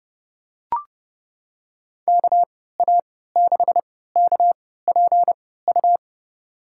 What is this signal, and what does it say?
Morse code, heard as a keyed tone, about 30 wpm, spaced as for 17 wpm, 700 Hz. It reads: KA6KPU
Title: USA Call Signs - 3x Character Spacing - 30wpm